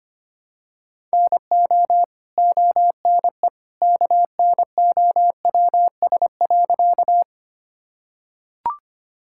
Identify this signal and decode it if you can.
Morse code, heard as a keyed tone, about 25 wpm, 700 Hz. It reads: NO ONE KNOWS.
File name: practice / 25wpm / Sentences with 2-Word Phrases - 25wpm